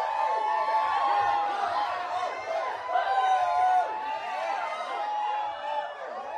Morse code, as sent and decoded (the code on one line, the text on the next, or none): none